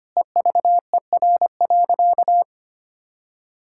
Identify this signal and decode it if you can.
Morse code, heard as a keyed tone, about 25 wpm, 700 Hz. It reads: EVER.
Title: Sentences from Top 1000 Words - Review Part 6 - 25wpm